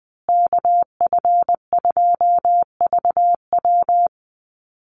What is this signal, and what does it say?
Morse code, heard as a keyed tone, about 20 wpm, 700 Hz. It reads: KF2VW
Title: USA Call Signs Repeated 3x 20wpm